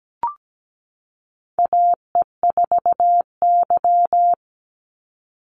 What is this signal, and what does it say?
Morse code, heard as a keyed tone, about 17 wpm, 700 Hz. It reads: AE4Y
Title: NA SOTA Chaser Callsigns 17wpm